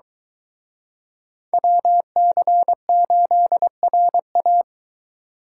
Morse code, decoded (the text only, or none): WC8RA